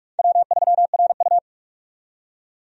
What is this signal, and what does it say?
Morse code, heard as a keyed tone, about 42 wpm, 700 Hz. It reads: W3RU
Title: USA Call Signs - 4 Characters - 45wpm